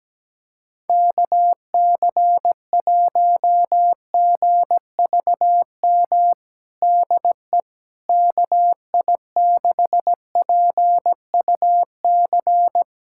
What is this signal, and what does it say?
Morse code, heard as a keyed tone, about 17 wpm, 700 Hz. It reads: KC1GVM DE KI6PUC
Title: USA Callsign DE Callsign 17wpm